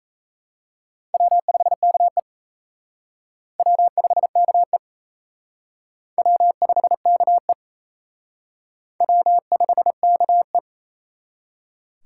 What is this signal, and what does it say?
Morse code, heard as a keyed tone, about 35 wpm, 700 Hz. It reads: W5KE W5KE W5KE W5KE